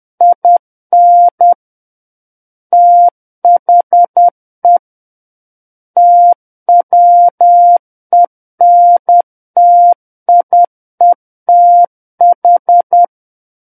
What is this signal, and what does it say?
Morse code, heard as a keyed tone, about 10 wpm, 700 Hz. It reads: IN THE TWENTIETH
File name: Book One / 01-Chapter 1 - 10 WPM